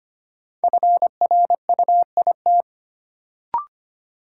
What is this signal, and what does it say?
Morse code, heard as a keyed tone, about 25 wpm, 700 Hz. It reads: FRUIT
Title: Sentences from Top 1300 Words - New Word Drill Alt - 25wpm